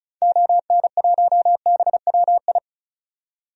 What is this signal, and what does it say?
Morse code, heard as a keyed tone, about 34 wpm, 700 Hz. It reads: ON1BWI
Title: DXCC-100 Call Signs - 6 Characters - 35wpm